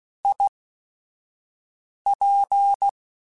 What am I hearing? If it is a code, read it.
Morse code, heard as a keyed tone, about 16 wpm, spaced as for 5 wpm, 800 Hz. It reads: IP